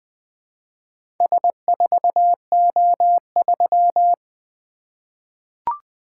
Morse code, heard as a keyed tone, about 20 wpm, 700 Hz. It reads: S4O3
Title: Four Letter-Number Combinations - 20wpm